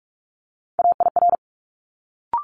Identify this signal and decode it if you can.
Morse code, heard as a keyed tone, about 46 wpm, 700 Hz. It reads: AIR
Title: Multi-letter Phonemes Quick Fire 45wpm